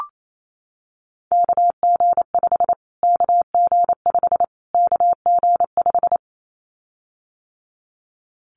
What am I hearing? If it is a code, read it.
Morse code, heard as a keyed tone, about 28 wpm, 700 Hz. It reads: KG5 KG5 KG5
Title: Callsign Prefixes for Top 25 DXCC - Repeated 3x - 28wpm